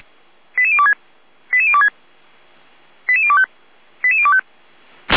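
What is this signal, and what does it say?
Five-tone selective call (ZVEI), heard as quick runs of five tones, tones of about 70 ms. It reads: 88026 88026 88025 88025
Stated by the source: ZVEI five-tone selective call received by SDR, logged by Feuerwehr Orlamünde fire brigade.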